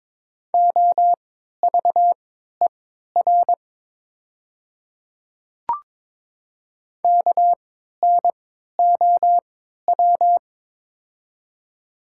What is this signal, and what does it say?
Morse code, heard as a keyed tone, about 23 wpm, spaced as for 13 wpm, 700 Hz. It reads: OVER KNOW
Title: 3x Character Spacing - Top 100 Words - 22wpm